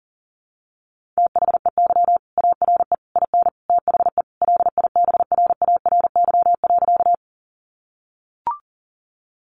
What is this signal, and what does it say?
Morse code, heard as a keyed tone, about 40 wpm, 700 Hz. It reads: THEY ARE IN THE LIBRARY.
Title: Sentences from Top 600 Words 40wpm